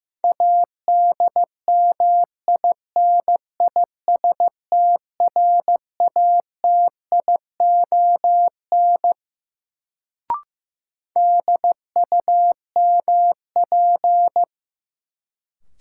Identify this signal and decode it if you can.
Morse code, heard as a keyed tone, about 16 wpm, 700 Hz. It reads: ADMINISTRATION DUMP
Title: Sentences from Top 1300 Words - New Word Drill - 15wpm